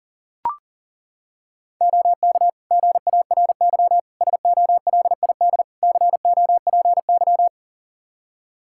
Morse code, heard as a keyed tone, about 40 wpm, 700 Hz. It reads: OK GARY SOLID COPY
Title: Practice QSO - Exchange 3 - More Info - 40wpm